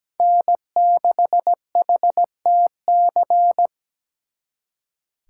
Morse code, heard as a keyed tone, about 20 wpm, 700 Hz. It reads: N6HTC
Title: Speed Racing - USA 1x3 Call Signs 17wpm